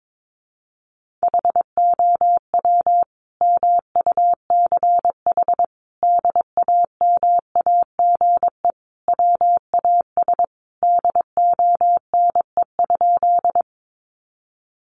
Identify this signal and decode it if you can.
Morse code, heard as a keyed tone, about 22 wpm, 700 Hz. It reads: HOW MUCH DAMAGE WAS DONE?
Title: Sentences from Top 1100 Words 22wpm